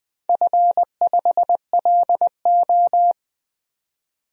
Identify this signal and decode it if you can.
Morse code, heard as a keyed tone, about 20 wpm, 700 Hz. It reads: F5LO
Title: DXCC-100 Call Signs - 4 Characters - 20wpm